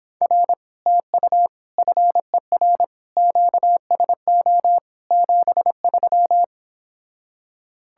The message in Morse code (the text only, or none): R TU FER QSO 73